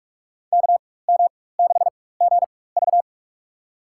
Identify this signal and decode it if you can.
Morse code, heard as a keyed tone, about 47 wpm, spaced as for 21 wpm, 700 Hz. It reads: KM6GV
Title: USA Call Signs - 5 Characters - 4x Character Spacing - 45wpm